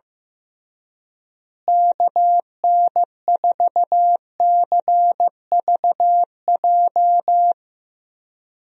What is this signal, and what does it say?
Morse code, heard as a keyed tone, about 15 wpm, 700 Hz. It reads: KN4CVJ
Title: USA Call Signs - 6 Characters - Repeated 2x - 15wpm